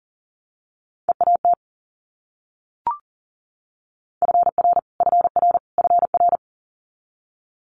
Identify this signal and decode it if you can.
Morse code, heard as a keyed tone, about 40 wpm, 700 Hz. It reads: EAT FR FR FR
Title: Multi-letter Phonemes Repeated 3x 40wpm